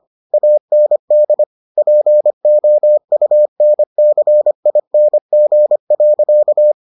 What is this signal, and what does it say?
Morse code, heard as a keyed tone, about 25 wpm, 600 Hz. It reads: AND POUNCING.